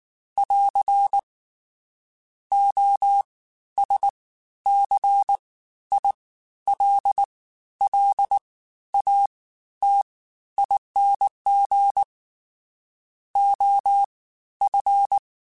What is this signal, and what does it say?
Morse code, heard as a keyed tone, about 19 wpm, spaced as for 12 wpm, 800 Hz. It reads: <AR> OSCILLATING OF
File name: Special INGs-mp3s / Special INGs-mp3s / Fast ing @18